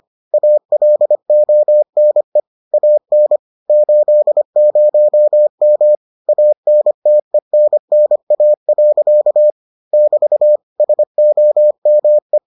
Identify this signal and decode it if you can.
Morse code, heard as a keyed tone, about 25 wpm, 600 Hz. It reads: ALONE AN 80M ANTENNA. <BT> SOME